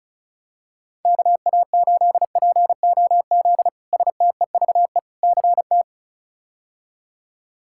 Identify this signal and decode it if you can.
Morse code, heard as a keyed tone, about 34 wpm, 700 Hz. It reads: KA8POZ STEVE CT
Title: K1USN SST QSO Practice - 35wpm